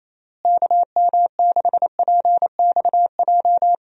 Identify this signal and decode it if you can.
Morse code, heard as a keyed tone, about 28 wpm, 700 Hz. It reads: KM6PXJ
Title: USA Call Signs - 6 Characters - Repeated 3x - 28wpm